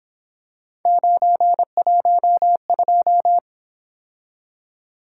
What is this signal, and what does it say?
Morse code, heard as a keyed tone, about 26 wpm, 700 Hz. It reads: 912
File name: Speed-Racing - Three Letter-Number Combinations - 22wpm